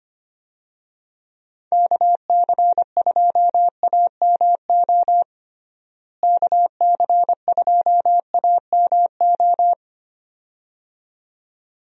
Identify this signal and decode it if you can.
Morse code, heard as a keyed tone, about 25 wpm, 700 Hz. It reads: KC2AMO KC2AMO
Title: USA Call Signs - 6 Characters - Repeated 2x - 25wpm